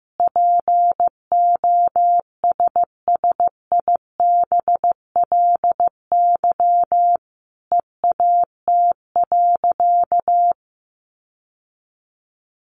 Morse code, heard as a keyed tone, about 15 wpm, 700 Hz. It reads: POSSIBLY EAT.
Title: Sentences from Top 1300 Words 15wpm